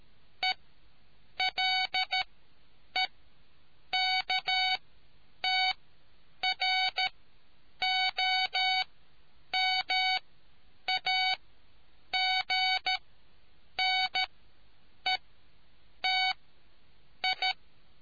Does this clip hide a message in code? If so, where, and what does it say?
Yes: Morse code from 0.4 to 17.5 s, heard as a keyed tone, about 13 wpm, spaced as for 8 wpm, 750 Hz. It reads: ELEKTROMAGNETI